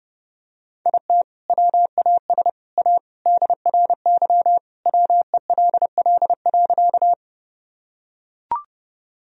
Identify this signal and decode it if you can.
Morse code, heard as a keyed tone, about 30 wpm, 700 Hz. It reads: IT WAS A DRY WELL.
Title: Sentences from Top 1000 Words - Review Part 3 - 30wpm